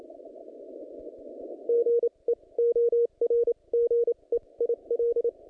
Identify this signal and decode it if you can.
Morse code, heard as a keyed tone, about 29 wpm, spaced as for 23 wpm, 470 Hz. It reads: GEORGE IL
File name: contests / naqp-cw-k5tr-op-k5pi- -0100